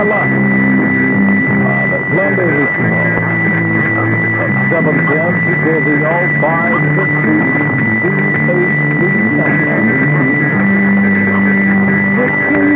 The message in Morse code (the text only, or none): none